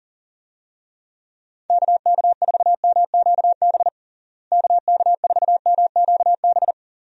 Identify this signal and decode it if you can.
Morse code, heard as a keyed tone, about 40 wpm, 700 Hz. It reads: KK4MQB KK4MQB
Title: USA Call Signs - 6 Characters - Repeated 2x - 40wpm